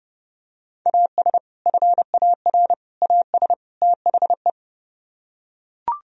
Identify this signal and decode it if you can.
Morse code, heard as a keyed tone, about 30 wpm, 700 Hz. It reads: AS FAR AS THE